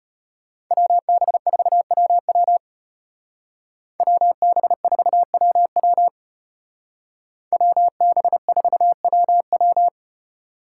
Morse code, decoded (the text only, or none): WB4WW WB4WW WB4WW